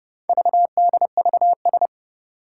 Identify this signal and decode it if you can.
Morse code, heard as a keyed tone, about 31 wpm, 700 Hz. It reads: VDVS
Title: Four Challenging Letter-Number Combinations 30wpm